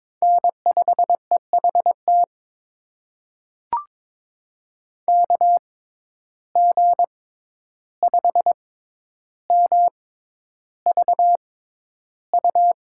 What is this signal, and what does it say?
Morse code, heard as a keyed tone, about 22 wpm, spaced as for 11 wpm, 700 Hz. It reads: N5EHT KG5MVU
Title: USA Call Signs - 6x Character Spacing - 22wpm